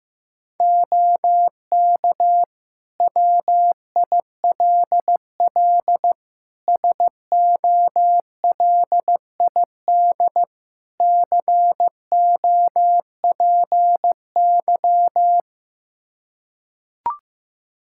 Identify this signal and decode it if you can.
Morse code, heard as a keyed tone, about 16 wpm, 700 Hz. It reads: OK WILL SOLID COPY
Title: Practice QSO - Exchange 3 - More Info - 15wpm